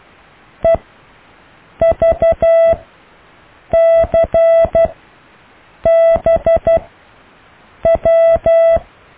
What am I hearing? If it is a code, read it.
Morse code, heard as a keyed tone, about 12 wpm, spaced as for 6 wpm, 660 Hz. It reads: EVCBW
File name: cw / Alle letters